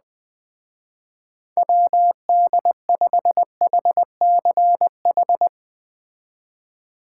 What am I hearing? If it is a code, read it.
Morse code, heard as a keyed tone, about 21 wpm, 700 Hz. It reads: WD5HCH